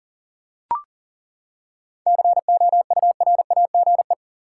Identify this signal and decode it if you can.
Morse code, heard as a keyed tone, about 40 wpm, 700 Hz. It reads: COURAGE
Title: Seven Letter Words - 40wpm